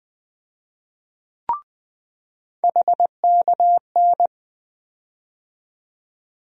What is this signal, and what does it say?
Morse code, heard as a keyed tone, about 20 wpm, 700 Hz. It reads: HKN
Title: USA Call Sign Postfixes - 3 Letters 20wpm